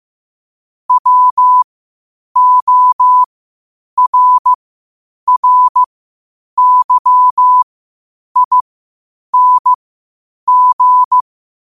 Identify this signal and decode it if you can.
Morse code, heard as a keyed tone, about 15 wpm, spaced as for 8 wpm, 1000 Hz. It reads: WORRYING
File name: qotdfiles / 8wpm